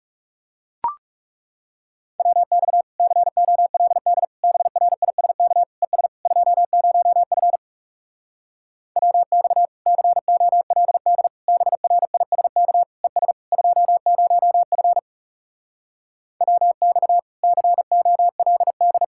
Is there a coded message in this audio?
Yes: Morse code from 2.2 to 19.0 s, heard as a keyed tone, about 40 wpm, 700 Hz. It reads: WX COLD BRISK ES 20F WX COLD BRISK ES 20F WX COLD